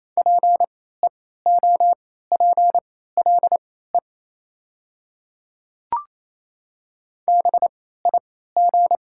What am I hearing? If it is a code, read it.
Morse code, heard as a keyed tone, about 28 wpm, spaced as for 16 wpm, 700 Hz. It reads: PEOPLE BIG
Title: 3x Character Spacing - Top 200 Words - 28wpm